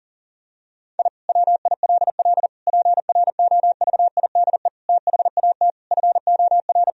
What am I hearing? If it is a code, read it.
Morse code, heard as a keyed tone, about 40 wpm, 700 Hz. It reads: I WILL PROVIDE THAT FOR
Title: Sentences from Top 1000 Words - Review Part 3 - 40wpm